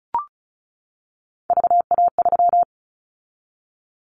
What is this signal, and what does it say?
Morse code, heard as a keyed tone, about 36 wpm, 700 Hz. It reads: VA3
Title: Callsign Prefixes for Top 25 DXCC - 35wpm